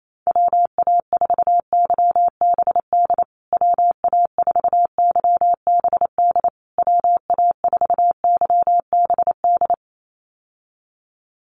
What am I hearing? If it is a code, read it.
Morse code, heard as a keyed tone, about 28 wpm, 700 Hz. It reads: WA4YBD WA4YBD WA4YBD
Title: USA Call Signs Repeated 3x 28wpm